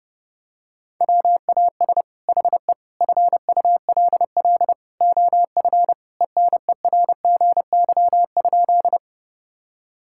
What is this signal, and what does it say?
Morse code, heard as a keyed tone, about 31 wpm, 700 Hz. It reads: WAS HE FULL OF ENERGY?
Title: Sentences from Top 1000 Words - Review Part 6 - 30wpm